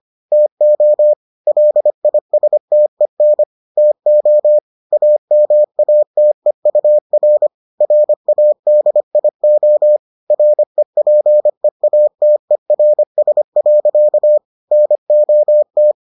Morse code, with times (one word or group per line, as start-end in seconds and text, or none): 0.3-1.1 TO
1.5-3.4 LISTEN
3.8-4.6 TO
4.9-7.5 AMATEUR
7.8-10.0 RADIO
10.3-14.4 REPEATERS.
14.7-15.9 NOT